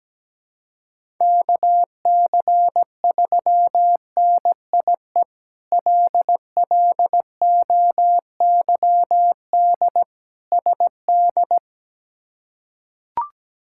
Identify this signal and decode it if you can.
Morse code, heard as a keyed tone, about 17 wpm, 700 Hz. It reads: KC3NIE LLOYD SD
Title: K1USN SST QSO Practice - 17wpm